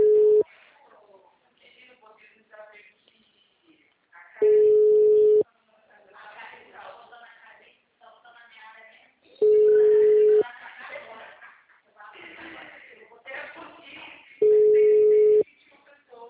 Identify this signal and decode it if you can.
Ringback tone (Europe, 425 Hz), with speech.
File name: out- -2032- - - .24